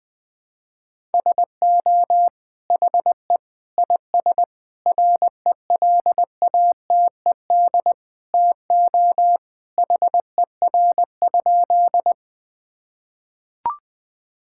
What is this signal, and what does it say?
Morse code, heard as a keyed tone, about 21 wpm, 700 Hz. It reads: SO HE IS RELATED TO HER?
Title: Sentences from Top 1000 Words - Review Part 5 - 20wpm